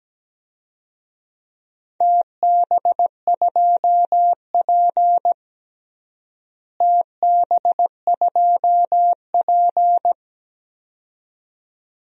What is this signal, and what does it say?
Morse code, heard as a keyed tone, about 17 wpm, 700 Hz. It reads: TB2P TB2P
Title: DXCC-100 Call Signs - 4 Characters - Repeated 2x - 17wpm